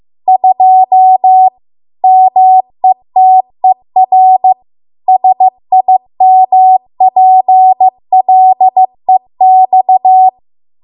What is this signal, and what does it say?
Morse code, heard as a keyed tone, about 15 wpm, 750 Hz. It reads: 2 METER SIMPLEX